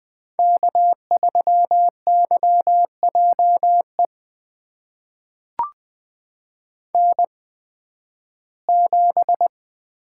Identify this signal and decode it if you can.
Morse code, heard as a keyed tone, about 20 wpm, spaced as for 13 wpm, 700 Hz. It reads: K3YJE N7
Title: USA Call Signs - 5 Characters - 8x Character Spacing - 20wpm